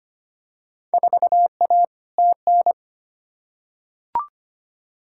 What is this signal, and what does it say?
Morse code, heard as a keyed tone, about 25 wpm, 700 Hz. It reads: 4A TN
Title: Field Day Exchanges - 25wpm